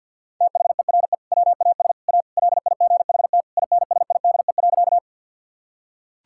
Morse code, heard as a keyed tone, about 50 wpm, 700 Hz. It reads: THERE WAS A LIGHT INSIDE.
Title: Sentences from Top 1000 Words - Review Part 6 - 50wpm